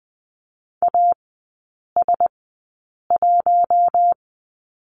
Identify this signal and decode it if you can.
Morse code, heard as a keyed tone, about 20 wpm, spaced as for 8 wpm, 700 Hz. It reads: AS1
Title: 041 Lesson 8e - Number 1 - Three Characters 20wpm